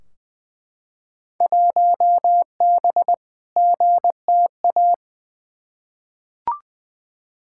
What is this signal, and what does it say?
Morse code, heard as a keyed tone, about 20 wpm, 700 Hz. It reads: 1B GTA
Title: Field Day Exchanges Encore - 20wpm